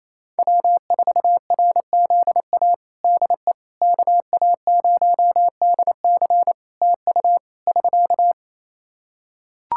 Morse code, heard as a keyed tone, about 28 wpm, 700 Hz. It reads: W4RZA DE KA0DC TU <SK>
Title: Practice QSO - Exchange 4 - Ending - 28wpm